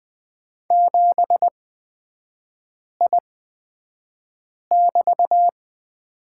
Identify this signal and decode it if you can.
Morse code, heard as a keyed tone, about 20 wpm, spaced as for 5 wpm, 690 Hz. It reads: 7I<BT>
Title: 229 Lesson 37e - Letter x - Three Characters 20wpm